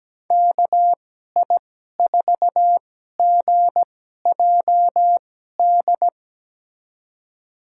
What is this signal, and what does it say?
Morse code, heard as a keyed tone, about 17 wpm, spaced as for 12 wpm, 700 Hz. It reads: KI4GJD
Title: USA Call Signs - 6 Characters - 2x Character Spacing - 17wpm